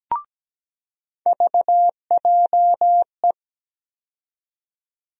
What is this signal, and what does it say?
Morse code, heard as a keyed tone, about 17 wpm, 700 Hz. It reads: VJE
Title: USA Call Sign Postfixes - 3 Letters 17wpm